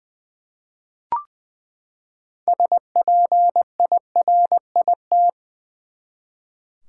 Morse code, heard as a keyed tone, about 20 wpm, 700 Hz. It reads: SPIRIT
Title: Six Letter Words - 20wpm